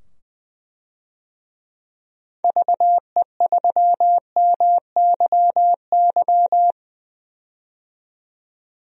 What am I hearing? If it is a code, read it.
Morse code, heard as a keyed tone, about 20 wpm, 700 Hz. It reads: VE3MYY
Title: DXCC-100 Call Signs - 6 Characters - 20wpm